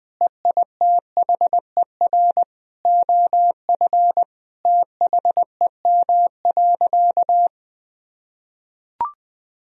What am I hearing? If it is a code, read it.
Morse code, heard as a keyed tone, about 21 wpm, 700 Hz. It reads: EITHER OF THEM.
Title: Sentences from Top 1000 Words - Review Part 4 - 20wpm